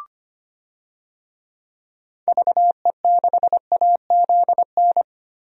Morse code, heard as a keyed tone, about 25 wpm, 700 Hz. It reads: VE6AZN
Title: DXCC-100 Call Signs - 6 Characters - 25wpm